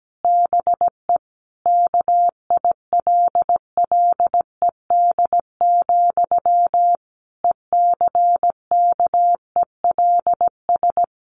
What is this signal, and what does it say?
Morse code, heard as a keyed tone, about 17 wpm, 700 Hz. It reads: BE KILLED, ECKELS
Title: A Sound Of Thunder 17wpm